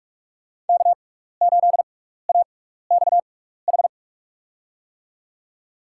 Morse code, heard as a keyed tone, about 47 wpm, spaced as for 16 wpm, 700 Hz. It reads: K8AXH
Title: USA Call Signs - 5 Characters - 6x Character Spacing - 45wpm